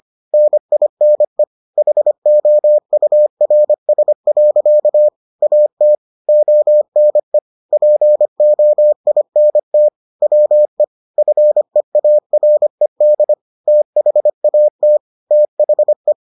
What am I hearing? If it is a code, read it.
Morse code, heard as a keyed tone, about 25 wpm, 600 Hz. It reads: NINE HOURS. AT ONE POINT WE FEARED THAT THE